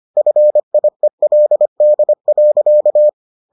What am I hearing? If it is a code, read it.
Morse code, heard as a keyed tone, about 25 wpm, 600 Hz. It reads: FIELD.